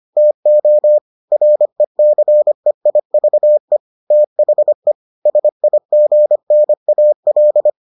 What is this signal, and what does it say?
Morse code, heard as a keyed tone, about 25 wpm, 600 Hz. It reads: TO RECEIVE THE SIGNAL